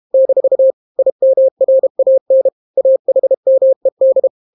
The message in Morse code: - IMRAN AHMED